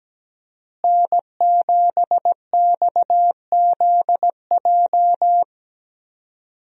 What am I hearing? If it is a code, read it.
Morse code, heard as a keyed tone, about 17 wpm, 700 Hz. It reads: N7XZJ
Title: USA Call Signs - 5 Characters - Repeated 3x - 17wpm